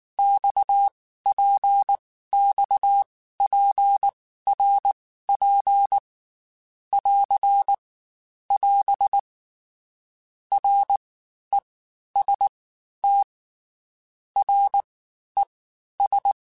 Morse code, heard as a keyed tone, about 19 wpm, 800 Hz. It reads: X P X P R P <AR> <AS> R E S T R E S
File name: Basic Letter Groups By7Groups / RKPX / RKPX@18